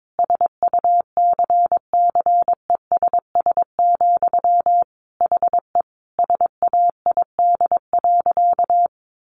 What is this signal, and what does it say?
Morse code, heard as a keyed tone, about 22 wpm, 700 Hz. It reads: SUCCESS, HE SAID.
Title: Sentences from Top 900 Words 22wpm